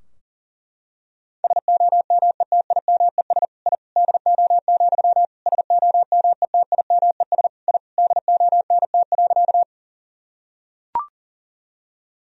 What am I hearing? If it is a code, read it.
Morse code, heard as a keyed tone, about 40 wpm, 700 Hz. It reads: SOMETIMES I DO, SOMETIMES I DONT.
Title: Sentences from Top 1000 Words - Review Part 3 - 40wpm